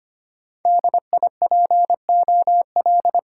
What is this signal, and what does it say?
Morse code, heard as a keyed tone, about 29 wpm, 700 Hz. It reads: DIPOL
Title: Speed-Racing Top 90 QSO Elements 25wpm